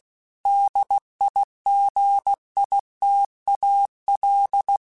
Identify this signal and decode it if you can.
Morse code, heard as a keyed tone, about 16 wpm, 800 Hz. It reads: DIGITAL
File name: Technical Subjects / Technical Subjects-mp3s / Gates Intro @15